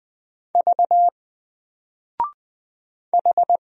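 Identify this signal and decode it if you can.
Morse code, heard as a keyed tone, about 20 wpm, 700 Hz. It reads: V H
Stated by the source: Single letter and number Morse drill at 20 wpm, from N4LCD.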